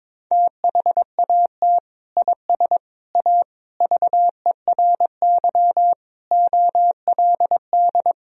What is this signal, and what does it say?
Morse code, heard as a keyed tone, about 21 wpm, 700 Hz. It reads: THAT IS A VERY OLD
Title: Sentences from Top 1000 Words - Review Part 1 - 22wpm